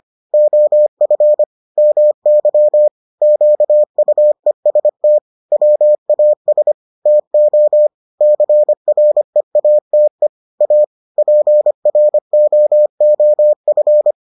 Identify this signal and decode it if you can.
Morse code, heard as a keyed tone, about 25 wpm, 600 Hz. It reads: OF MY QUEST WAS TO CREATE A PROOF